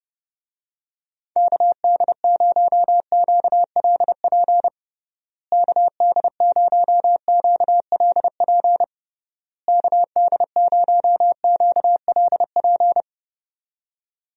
Morse code, heard as a keyed tone, about 30 wpm, 700 Hz. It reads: KD0QLP KD0QLP KD0QLP
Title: USA Call Signs - 6 Characters - Repeated 3x - 30wpm